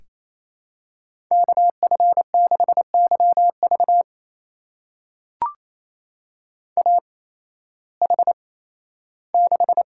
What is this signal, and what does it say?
Morse code, heard as a keyed tone, about 28 wpm, spaced as for 15 wpm, 700 Hz. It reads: KF6YV AH6
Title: USA Call Signs - 5 Characters - 8x Character Spacing - 28wpm